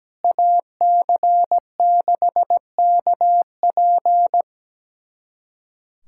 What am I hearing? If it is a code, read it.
Morse code, heard as a keyed tone, about 17 wpm, 700 Hz. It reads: AC6KP
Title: USA Callsign DE Callsign - 5 Characters - 17wpm